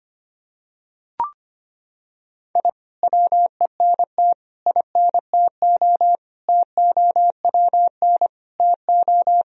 Morse code, heard as a keyed tone, about 25 wpm, 700 Hz. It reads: I WENT INTO TOWN TO
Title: Sentences from Top 1000 Words - Review Part 4 - 25wpm